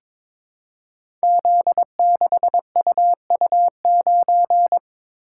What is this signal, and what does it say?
Morse code, heard as a keyed tone, about 22 wpm, 700 Hz. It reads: Z6UU9
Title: Code Groups - 5 Letter:Numbers - 22wpm